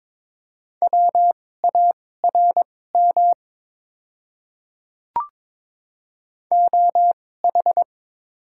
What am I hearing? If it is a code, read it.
Morse code, heard as a keyed tone, about 22 wpm, spaced as for 16 wpm, 700 Hz. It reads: WARM OH
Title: 2x Character Spacing - Top 500 Words - 22wpm